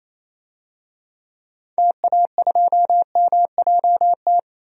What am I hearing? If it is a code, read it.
Morse code, heard as a keyed tone, about 28 wpm, 700 Hz. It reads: TA2MJT
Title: DXCC-100 Call Signs - 6 Characters - 28wpm